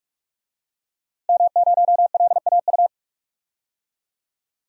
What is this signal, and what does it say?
Morse code, heard as a keyed tone, about 46 wpm, 700 Hz. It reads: M0LAU
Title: DXCC-100 Call Signs - 5 Characters - 45wpm